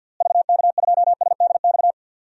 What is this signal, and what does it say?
Morse code, heard as a keyed tone, about 51 wpm, 700 Hz. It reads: VK2SDX
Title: DXCC-100 Call Signs - 6 Characters - Repeated 3x - 50wpm